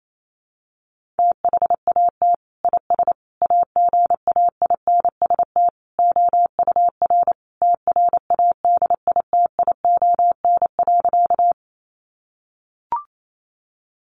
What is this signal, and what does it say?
Morse code, heard as a keyed tone, about 28 wpm, 700 Hz. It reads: THAT IS AGAINST OUR TRADITION.
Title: Sentences from Top 1500 Words 28wpm